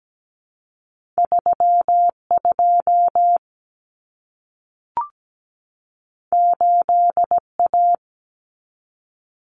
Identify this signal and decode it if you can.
Morse code, heard as a keyed tone, about 17 wpm, 700 Hz. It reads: 3 2 8 A
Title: Common Cut Numbers - 2-Digit 17wpm